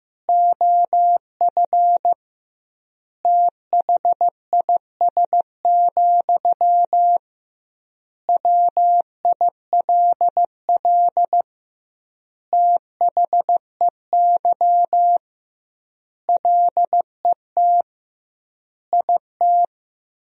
Morse code, heard as a keyed tone, about 15 wpm, 700 Hz. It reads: OF THIS, WILL THEY LET IT
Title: 2x Word-Spacing - Sentences from Top 400 Words - 15wpm